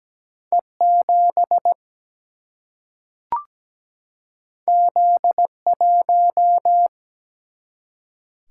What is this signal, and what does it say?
Morse code, heard as a keyed tone, about 17 wpm, 700 Hz. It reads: E7 Z1
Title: Single-Letter Single-Number 17wpm